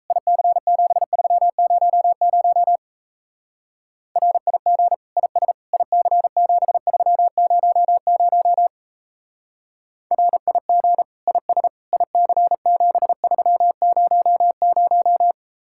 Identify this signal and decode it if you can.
Morse code, heard as a keyed tone, about 34 wpm, 700 Hz. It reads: IC7300 RIG IS IC7300 RIG IS IC7300
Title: Speed-Racing Common QSO Phrases - 28wpm